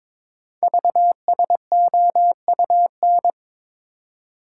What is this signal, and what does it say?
Morse code, heard as a keyed tone, about 22 wpm, 700 Hz. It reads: VSOUN
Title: Code Groups - 5 Letters - 22wpm